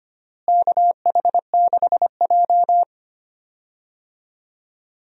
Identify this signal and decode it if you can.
Morse code, heard as a keyed tone, about 25 wpm, 700 Hz. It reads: KH6J
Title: DXCC-100 Call Signs - 4 Characters - Repeated 2x - 25wpm